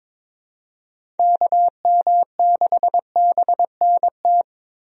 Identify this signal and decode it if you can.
Morse code, heard as a keyed tone, about 22 wpm, 700 Hz. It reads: KM6BNT
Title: USA Call Signs - 6 Characters - Repeated 2x - 22wpm